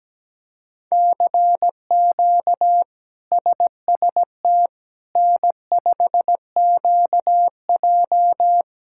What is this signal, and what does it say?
Morse code, heard as a keyed tone, about 17 wpm, 700 Hz. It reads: CQ SST N5QJ
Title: K1USN SST QSO Practice - 17wpm